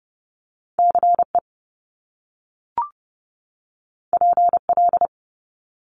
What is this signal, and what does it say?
Morse code, heard as a keyed tone, about 30 wpm, 700 Hz. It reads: CE PL
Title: Top Two Letter Combinations in Words 30wpm